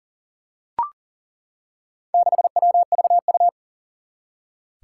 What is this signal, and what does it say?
Morse code, heard as a keyed tone, about 40 wpm, 700 Hz. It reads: 6WVU
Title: Four Challenging Letter-Number Combinations 40wpm